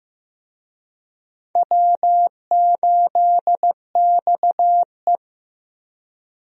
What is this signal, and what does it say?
Morse code, heard as a keyed tone, about 15 wpm, 700 Hz. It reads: W8XE
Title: USA Call Signs - 4 Characters - Repeated 2x - 15wpm